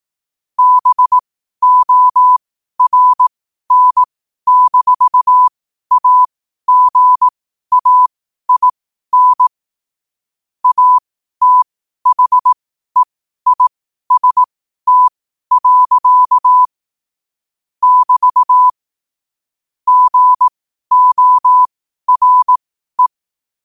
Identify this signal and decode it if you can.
Morse code, heard as a keyed tone, about 18 wpm, spaced as for 12 wpm, 1000 Hz. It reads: BORN-AGAIN ATHEIST. <BT> GORE
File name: qotdfiles / 12wpm